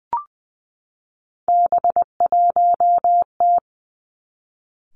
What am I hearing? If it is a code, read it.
Morse code, heard as a keyed tone, about 20 wpm, 700 Hz. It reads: B1T
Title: Three Challenging Letter-Number Combinations 20wpm